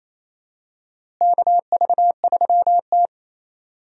Morse code, heard as a keyed tone, about 28 wpm, 700 Hz. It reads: KV3T
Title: USA Call Signs - 4 Characters - Repeated 2x - 28wpm